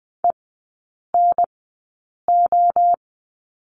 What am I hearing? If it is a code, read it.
Morse code, heard as a keyed tone, about 20 wpm, spaced as for 8 wpm, 700 Hz. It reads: ENO
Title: 053 Lesson 10e - Letter r - Three Characters 20wpm